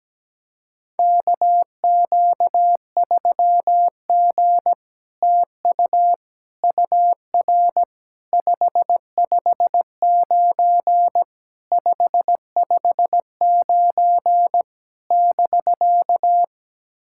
Morse code, heard as a keyed tone, about 17 wpm, 700 Hz. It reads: KQ3G TU UR 559 559 <BK>
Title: SOTA QSO Practice 17wpm